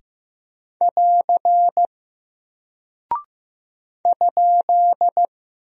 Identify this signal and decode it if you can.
Morse code, heard as a keyed tone, about 15 wpm, 700 Hz. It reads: <AR> ?
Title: Single Character (Letter:Number:Punctuation:Prosign) Alt 15wpm